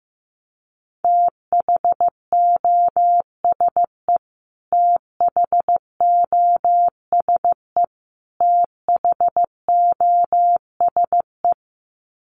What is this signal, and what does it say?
Morse code, heard as a keyed tone, about 15 wpm, 700 Hz. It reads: THOSE THOSE THOSE
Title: Top 500 Words Repeated 15wpm